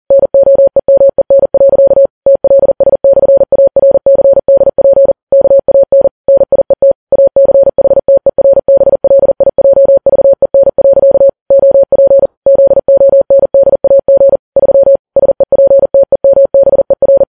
Morse code, heard as a keyed tone, about 40 wpm, 550 Hz. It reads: NOEMEN. TLS/ARCDP KAN NIET ACHTERBLIJVEN. OP ZONDAG 3 SEPTEMBER
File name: Files Web tls / Morse on9fv / morse40kps